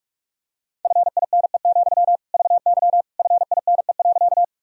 Morse code, heard as a keyed tone, about 41 wpm, 690 Hz. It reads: FINE, VY FINE.
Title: Sentences from Top 500 Words and Abbreviations 45wpm